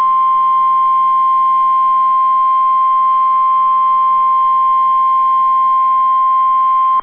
Emergency Alert System (SAME) tones (the 1050 Hz attention tone). It is heard from 0.6 to 6.8 s.